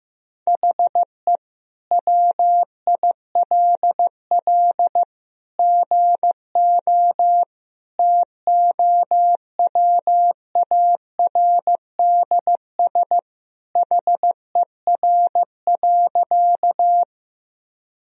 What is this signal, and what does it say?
Morse code, heard as a keyed tone, about 15 wpm, 700 Hz. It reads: HE WILL GO TOWARDS HER.
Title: Sentences from Top 1000 Words - Review Part 2 - 15wpm